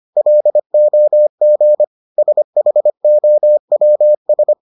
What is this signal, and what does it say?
Morse code, heard as a keyed tone, about 25 wpm, 600 Hz. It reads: LOG SHOWS